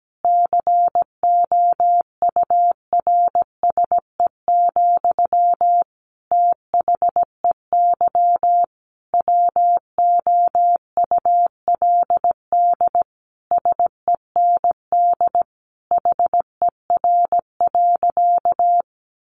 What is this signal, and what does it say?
Morse code, heard as a keyed tone, about 17 wpm, 700 Hz. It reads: COURSE, THEY WOULD SEND HER.
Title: Sentences from Top 1500 Words 17wpm